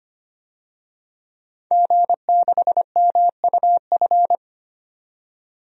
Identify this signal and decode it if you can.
Morse code, heard as a keyed tone, about 25 wpm, 700 Hz. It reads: G6MUF
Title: DXCC-100 Call Signs - 5 Characters - 25wpm